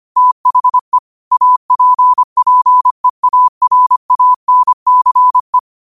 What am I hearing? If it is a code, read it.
Morse code, heard as a keyed tone, about 25 wpm, 1000 Hz. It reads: THE APPEARANCE